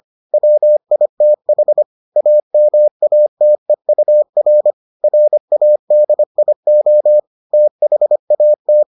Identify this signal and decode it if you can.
Morse code, heard as a keyed tone, about 25 wpm, 600 Hz. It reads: WITH AMATEUR RADIO THAT